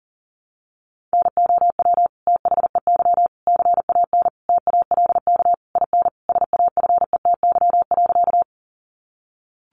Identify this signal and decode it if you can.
Morse code, heard as a keyed tone, about 40 wpm, 700 Hz. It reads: NOW THEY CAN TALK IN SAFETY.
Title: Sentences from Top 700 Words 40wpm